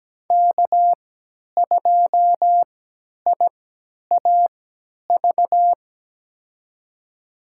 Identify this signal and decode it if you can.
Morse code, heard as a keyed tone, about 17 wpm, spaced as for 10 wpm, 700 Hz. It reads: K2IAV
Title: USA Call Signs - 5 Characters - 3x Character Spacing - 17wpm